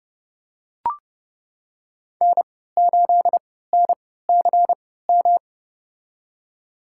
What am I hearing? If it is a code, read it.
Morse code, heard as a keyed tone, about 30 wpm, spaced as for 17 wpm, 700 Hz. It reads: N8NCM
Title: USA Call Signs - 5 Characters - 3x Character Spacing - 30wpm